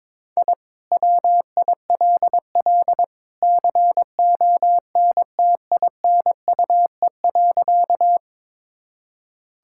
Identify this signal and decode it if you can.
Morse code, heard as a keyed tone, about 21 wpm, 700 Hz. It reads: I WILL CONTINUE.
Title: Sentences from Top 1000 Words - Review Part 6 - 22wpm